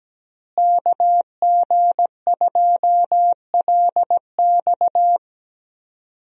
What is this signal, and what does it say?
Morse code, heard as a keyed tone, about 17 wpm, 700 Hz. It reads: KG2LX
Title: USA Call Signs - 5 Characters - Repeated 3x - 17wpm